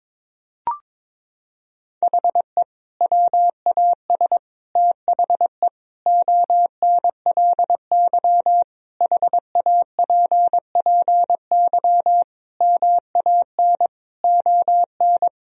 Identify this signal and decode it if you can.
Morse code, heard as a keyed tone, about 22 wpm, 700 Hz. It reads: HE WAS THE ONLY HAPPY MAN ON
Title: Sentences from Top 1000 Words - Review Part 3 - 22wpm